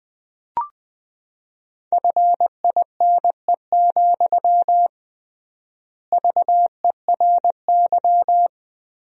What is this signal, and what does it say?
Morse code, heard as a keyed tone, about 21 wpm, 700 Hz. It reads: FINE, VERY
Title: 3x Word-Spacing - Sentences from Top 500 Words - 20wpm